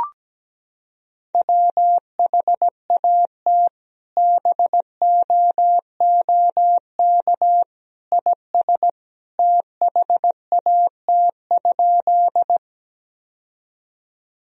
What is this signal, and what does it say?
Morse code, heard as a keyed tone, about 17 wpm, 700 Hz. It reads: WHAT BOOK IS THAT?
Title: Sentences from Top 1000 Words - Review Part 1 - 17wpm